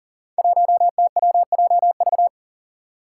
Morse code, heard as a keyed tone, about 40 wpm, 690 Hz. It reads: 1TWJV
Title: Five Challenging Letter-Number Combinations 40wpm